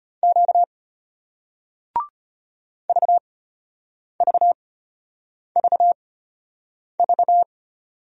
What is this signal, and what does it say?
Morse code, heard as a keyed tone, about 34 wpm, spaced as for 8 wpm, 700 Hz. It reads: Q VVVV